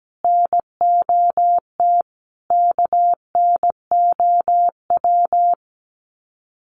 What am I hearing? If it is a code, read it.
Morse code, heard as a keyed tone, about 17 wpm, 700 Hz. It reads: NOT KNOW
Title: Sets of 5 Words Encore2 17wpm